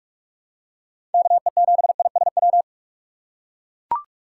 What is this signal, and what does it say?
Morse code, heard as a keyed tone, about 42 wpm, 700 Hz. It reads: KE7ISW
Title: USA Call Signs - 3x Character Spacing - 45wpm